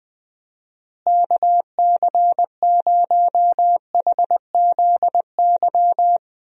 Morse code, heard as a keyed tone, about 20 wpm, 700 Hz. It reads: KC0HZY
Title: USA Call Signs - 6 Characters - Repeated 2x - 20wpm